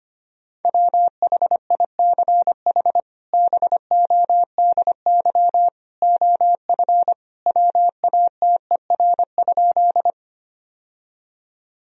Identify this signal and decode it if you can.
Morse code, heard as a keyed tone, about 25 wpm, 700 Hz. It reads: WHICH BODY OF WATER?
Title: Sentences from Top 1000 Words - Review Part 2 - 25wpm